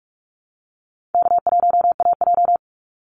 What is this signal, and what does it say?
Morse code, heard as a keyed tone, about 44 wpm, 700 Hz. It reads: K1AJ
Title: Call Signs Repeated 2x - 45wpm